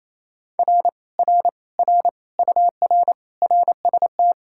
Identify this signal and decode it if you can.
Morse code, heard as a keyed tone, about 27 wpm, 700 Hz. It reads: R R R UR RST
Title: Speed-Racing Common QSO Phrases - 28wpm